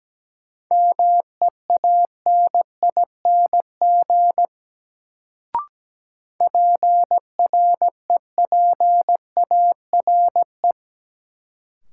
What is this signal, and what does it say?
Morse code, heard as a keyed tone, about 17 wpm, 700 Hz. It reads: MEANING PREPARE
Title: Sentences from Top 900 Words - New Word Drill - 17wpm